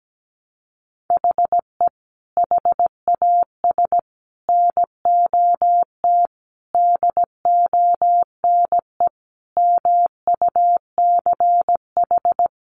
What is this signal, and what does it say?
Morse code, heard as a keyed tone, about 17 wpm, 700 Hz. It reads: HE HAS NOT DONE MUCH
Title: Sentences from Top 1100 Words 17wpm